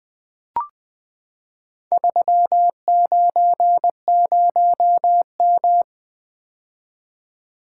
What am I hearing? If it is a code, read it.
Morse code, heard as a keyed tone, about 21 wpm, 700 Hz. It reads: 390M
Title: Four Letter-Number Combinations - 20wpm